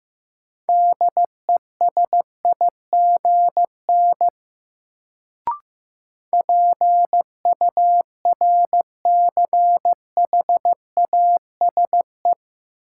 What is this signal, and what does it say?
Morse code, heard as a keyed tone, about 15 wpm, 700 Hz. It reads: DESIGN PURCHASE
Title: Sentences from Top 800 Words - New Word Drill - 15wpm